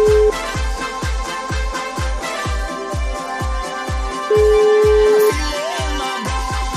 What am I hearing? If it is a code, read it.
Ringback tone (Europe, 425 Hz), with noise.